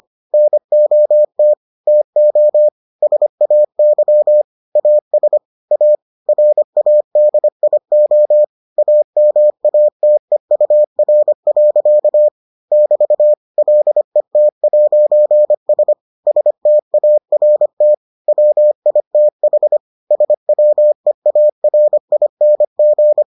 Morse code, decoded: NOT TO SAY AS A RADIO AMATEUR. <BT> LET'S START WITH SWEARING

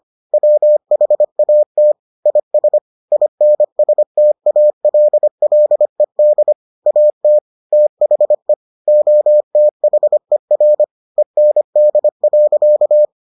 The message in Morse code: WHAT IS INSTALLED AT THE OTHER END.

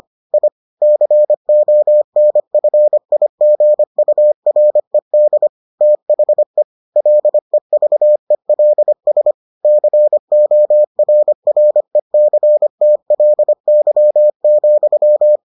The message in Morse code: I CONFIGURED THE LEVELS CORRECTLY,

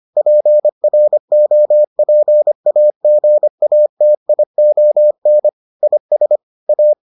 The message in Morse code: PROPAGATION IS A

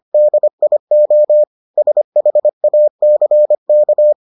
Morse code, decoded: DIO SHACK

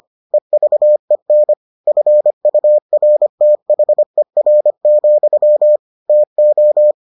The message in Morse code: EVEN FURTHER, TO

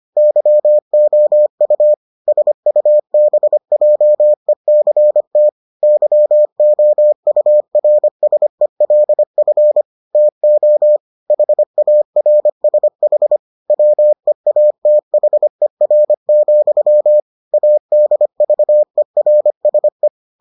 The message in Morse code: YOU SUBJECT YOURSELF TO HARSH WEATHER, ADVERSE

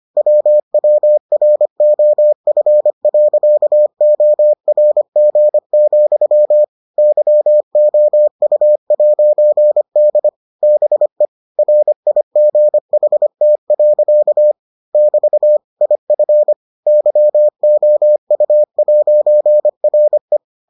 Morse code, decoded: WWROF.ORG, YOU'D BE RIGHT. <BT> IF YOU'RE

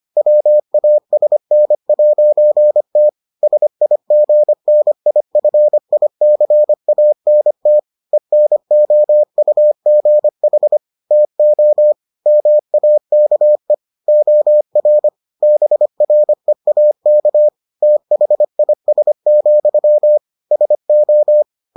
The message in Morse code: WASN'T SIGNIFICANT ENOUGH TO MAKE OR BREAK THIS, SO

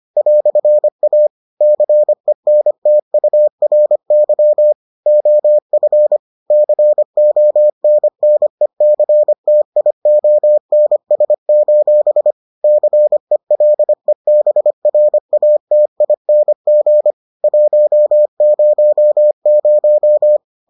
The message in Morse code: "A CENTURY OF CONNECTIONS: CELEBRATING 100

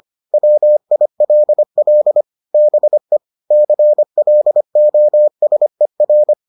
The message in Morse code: WILL BE CLOSER

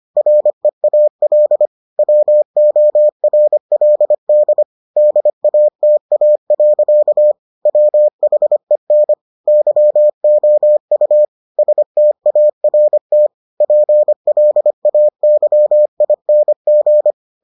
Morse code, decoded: REAL WORLD DATA. WHEN YOU START PLAYING

